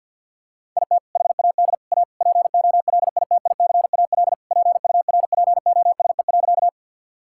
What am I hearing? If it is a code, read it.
Morse code, heard as a keyed tone, about 47 wpm, 700 Hz. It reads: IT HAD A POLITICAL PURPOSE.
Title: Sentences from Top 1000 Words - Review Part 5 - 50wpm